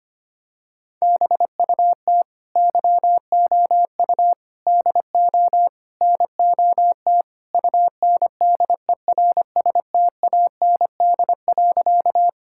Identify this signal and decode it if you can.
Morse code, heard as a keyed tone, about 25 wpm, 700 Hz. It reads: BUT YOU DO NOT UNDERSTAND.
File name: practice / 25wpm / Sentences with 4-Word Phrases - 25wpm